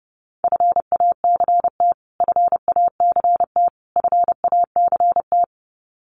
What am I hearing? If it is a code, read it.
Morse code, heard as a keyed tone, about 30 wpm, 700 Hz. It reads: FACT FACT FACT
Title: Top 500 Words Repeated 30wpm